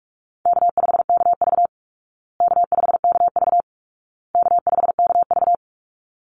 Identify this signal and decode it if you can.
Morse code, heard as a keyed tone, about 44 wpm, 700 Hz. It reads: K5KV K5KV K5KV
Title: Call Signs Repeated 2x - 45wpm